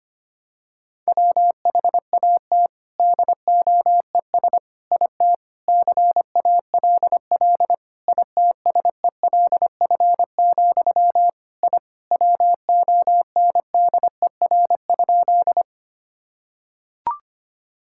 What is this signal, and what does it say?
Morse code, heard as a keyed tone, about 25 wpm, 700 Hz. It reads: WHAT DOES IT CALL ITSELF, I WONDER?
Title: Sentences from Top 1000 Words - Review Part 5 - 25wpm